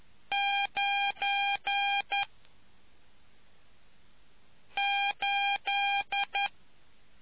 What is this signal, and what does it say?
Morse code, heard as a keyed tone, about 11 wpm, 800 Hz. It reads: 9 8